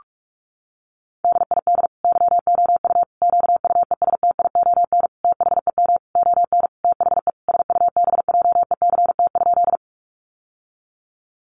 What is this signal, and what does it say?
Morse code, heard as a keyed tone, about 44 wpm, 700 Hz. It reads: DID YOU QUESTION THEM ON THE SUBJECT?
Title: Sentences from Top 800 Words 45wpm